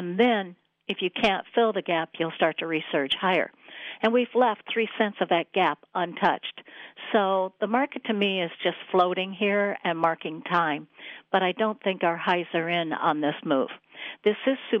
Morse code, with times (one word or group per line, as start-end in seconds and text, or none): none